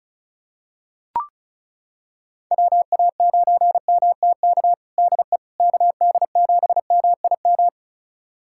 Morse code, heard as a keyed tone, about 36 wpm, 690 Hz. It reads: WA9MTK DE KD7MIM
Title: USA Callsign DE Callsign - 6 Characters - 35wpm